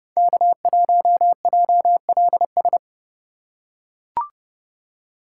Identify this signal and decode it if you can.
Morse code, heard as a keyed tone, about 32 wpm, 700 Hz. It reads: K1JLS
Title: USA Call Signs - 5 Characters - 4x Character Spacing - 30wpm